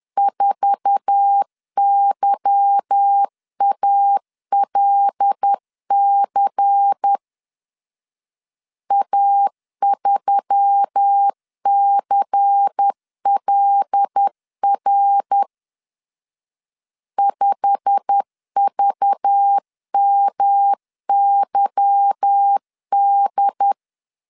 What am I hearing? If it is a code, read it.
Morse code, heard as a keyed tone, about 10 wpm, 800 Hz. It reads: 4YALC A3CLR 5VMYD
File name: LEZIONI / 8 YJ45 / 8 YJ45 10wpm 2549